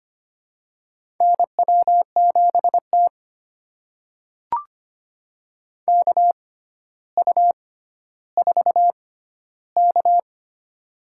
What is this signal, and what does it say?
Morse code, heard as a keyed tone, about 25 wpm, spaced as for 13 wpm, 700 Hz. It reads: NW7T KU4K
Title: USA Call Signs - 4 Characters - 6x Character Spacing - 25wpm